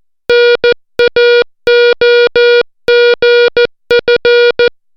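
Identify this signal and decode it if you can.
Morse code, heard as a keyed tone, about 14 wpm, 480 Hz. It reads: NAOGF
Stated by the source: Morse alphabet lesson, 60 characters per minute.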